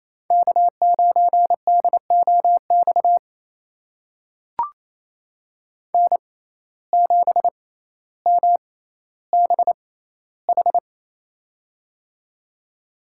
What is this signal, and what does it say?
Morse code, heard as a keyed tone, about 28 wpm, spaced as for 14 wpm, 700 Hz. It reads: K9DOX N7MBH
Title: USA Call Signs - 5 Characters - 6x Character Spacing - 28wpm